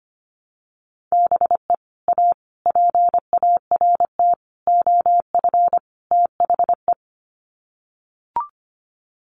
Morse code, heard as a keyed tone, about 25 wpm, 700 Hz. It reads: BE A PART OF THE
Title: Sets of 5 Words Encore 25wpm